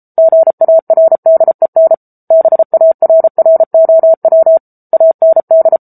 Morse code, heard as a keyed tone, about 34 wpm, 650 Hz. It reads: GARDEN BARROW AND
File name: Book Two / 07-Chapter 7 - 33 WPM